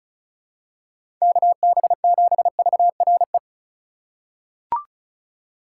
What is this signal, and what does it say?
Morse code, heard as a keyed tone, about 36 wpm, 700 Hz. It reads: KB7VRE
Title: USA Call Signs - 6 Characters - 4x Character Spacing - 35wpm